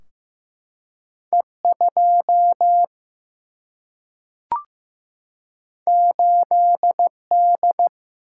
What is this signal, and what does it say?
Morse code, heard as a keyed tone, about 15 wpm, 700 Hz. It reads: E2 8D
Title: Single-Letter Single-Number 15wpm